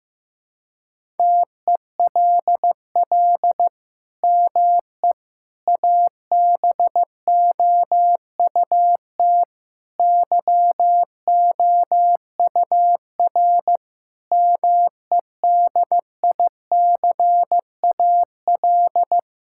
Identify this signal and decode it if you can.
Morse code, heard as a keyed tone, about 15 wpm, 700 Hz. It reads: TELL ME ABOUT YOUR MEDICAL
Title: Sentences from Top 1000 Words - Review Part 5 - 15wpm